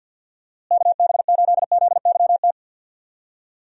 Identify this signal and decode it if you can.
Morse code, heard as a keyed tone, about 46 wpm, 700 Hz. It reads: KB8ZYT
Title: USA Call Signs - 6 Characters - Repeated 3x - 50wpm